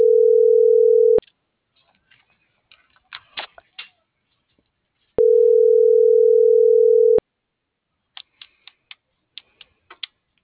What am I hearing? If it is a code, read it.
Ringback tone (North America), 440 + 480 Hz.